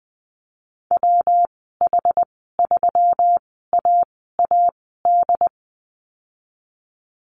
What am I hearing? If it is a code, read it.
Morse code, heard as a keyed tone, about 20 wpm, spaced as for 14 wpm, 700 Hz. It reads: WH3AAD
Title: 219 Lesson 35g - Number 0 - Call Signs 20wpm